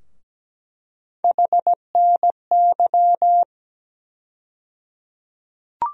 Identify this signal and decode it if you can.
Morse code, heard as a keyed tone, about 17 wpm, 700 Hz. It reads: HNY
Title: USA Call Sign Postfixes - 3 Letters 17wpm